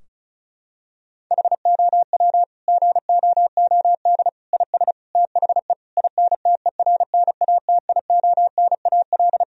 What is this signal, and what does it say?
Morse code, heard as a keyed tone, about 36 wpm, 700 Hz. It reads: HOW GOOD IS THE INTERNATIONAL
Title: Sentences from Top 1000 Words - Review Part 4 - 35wpm